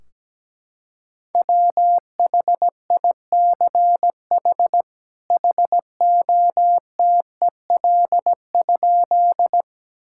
Morse code, heard as a keyed tone, about 17 wpm, 700 Hz. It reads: WHICH HOTEL?
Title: Sentences from Top 1600 Words 17wpm